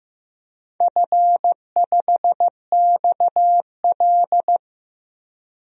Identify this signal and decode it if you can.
Morse code, heard as a keyed tone, about 15 wpm, 700 Hz. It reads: F5XL
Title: DXCC-100 Call Signs - 4 Characters - Repeated 3x - 15wpm